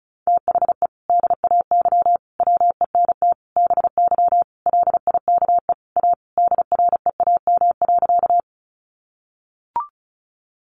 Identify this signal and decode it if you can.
Morse code, heard as a keyed tone, about 34 wpm, 700 Hz. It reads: THE DAY WENT BY LIKE A DREAM.
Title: Sentences from Top 1500 Words 35wpm